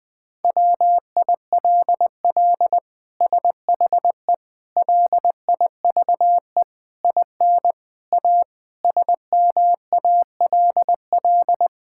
Morse code, heard as a keyed tone, about 20 wpm, 700 Hz. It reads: WILL SHE LIVE IN A SMALL
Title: Sentences from Top 1000 Words - Review Part 6 - 20wpm